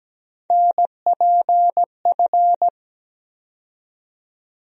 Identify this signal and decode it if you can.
Morse code, heard as a keyed tone, about 17 wpm, 700 Hz. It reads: NPF
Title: USA Call Sign Postfixes - 3 Letters 17wpm